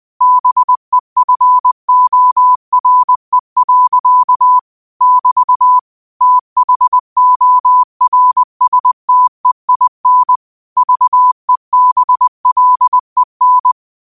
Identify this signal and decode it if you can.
Morse code, heard as a keyed tone, about 20 wpm, 1000 Hz. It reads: BEFORE. <BT> THORSTEIN VEBLEN
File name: qotdfiles / 20wpm